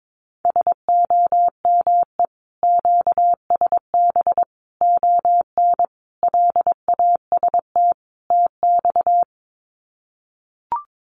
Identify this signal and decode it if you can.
Morse code, heard as a keyed tone, about 22 wpm, 700 Hz. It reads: SOME QSB ON LAST TX
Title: Common QSO Phrases 22wpm